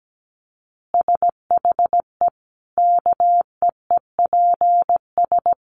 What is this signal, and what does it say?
Morse code, heard as a keyed tone, about 17 wpm, 700 Hz. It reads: SHE KEEPS
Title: Easy Phrases 17wpm